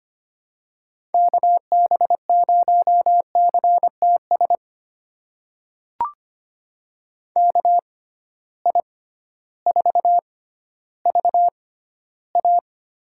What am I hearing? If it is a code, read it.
Morse code, heard as a keyed tone, about 25 wpm, spaced as for 13 wpm, 700 Hz. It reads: KB0CTS KI4VA
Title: USA Call Signs - 6 Characters - 6x Character Spacing - 25wpm